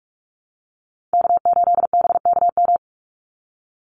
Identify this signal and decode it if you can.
Morse code, heard as a keyed tone, about 44 wpm, 700 Hz. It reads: K8BKM
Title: Call Signs Repeated 3x - 45wpm